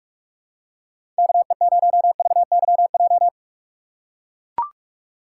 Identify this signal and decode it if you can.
Morse code, heard as a keyed tone, about 43 wpm, 700 Hz. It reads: KE0VYJ
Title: USA Call Signs - 6 Characters - 6x Character Spacing - 45wpm